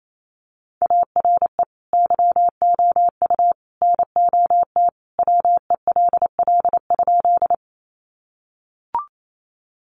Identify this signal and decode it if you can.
Morse code, heard as a keyed tone, about 28 wpm, 700 Hz. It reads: ARE YOU NOT WELL?
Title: Sentences from Top 600 Words 28wpm